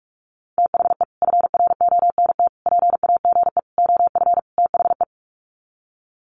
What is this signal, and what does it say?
Morse code, heard as a keyed tone, about 44 wpm, 700 Hz. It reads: THE FRONT PAGE OF THE
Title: Sets of 5 Words Encore 45wpm